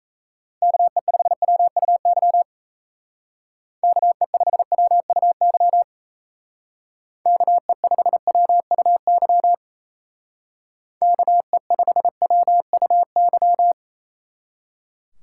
Morse code, heard as a keyed tone, about 35 wpm, 700 Hz. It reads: KE5WUY KE5WUY KE5WUY KE5WUY
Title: Speed Racing - USA 2x3 Call Signs 28wpm